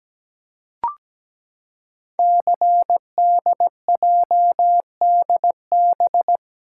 Morse code, heard as a keyed tone, about 17 wpm, 700 Hz. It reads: CDJDB
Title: Code Groups - 5 Letters - 17wpm